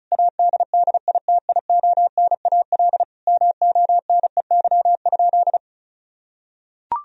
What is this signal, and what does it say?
Morse code, heard as a keyed tone, about 36 wpm, 700 Hz. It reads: ADDITIONAL MONEY?
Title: Sentences from Top 1000 Words - Review Part 4 - 35wpm